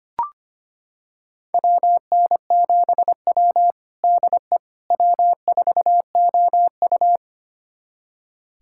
Morse code, heard as a keyed tone, about 25 wpm, 700 Hz. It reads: WN7W DE W4OU
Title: USA Callsign DE Callsign - 4 Characters - 25wpm